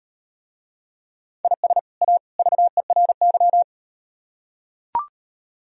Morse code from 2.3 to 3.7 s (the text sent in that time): VERY